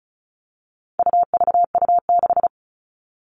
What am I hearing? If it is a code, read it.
Morse code, heard as a keyed tone, about 34 wpm, 700 Hz. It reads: UVU6
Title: All Cut Numbers - 4-Digit 35wpm